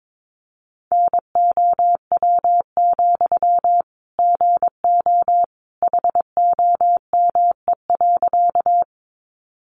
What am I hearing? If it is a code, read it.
Morse code, heard as a keyed tone, about 22 wpm, 700 Hz. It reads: NOW, GO HOME.
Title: Sentences from Top 200 Words 22wpm